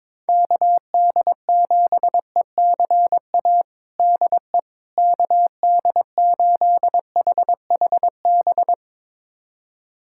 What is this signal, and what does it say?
Morse code, heard as a keyed tone, about 21 wpm, 700 Hz. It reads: KD7ECA DE KD8HHB
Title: USA Callsign DE Callsign - 6 Characters - 22wpm